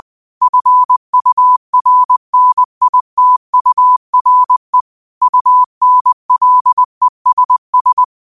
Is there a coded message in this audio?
Yes: Morse code from 0.4 to 8.0 s, heard as a keyed tone, about 20 wpm, 1000 Hz. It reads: FURNITURE UNLESS